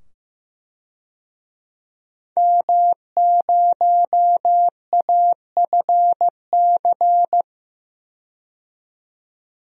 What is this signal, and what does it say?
Morse code, heard as a keyed tone, about 15 wpm, 700 Hz. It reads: M0AFC
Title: DXCC-100 Call Signs - 5 Characters - 15wpm